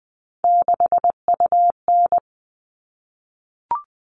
Morse code, heard as a keyed tone, about 20 wpm, 700 Hz. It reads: 6UN
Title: All Cut Numbers - 3-Digit 20wpm